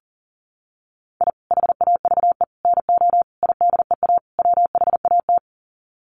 Morse code, heard as a keyed tone, about 40 wpm, 700 Hz. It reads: I HAVE NO IDEA WHAT
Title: Sets of 5 Words Encore 40wpm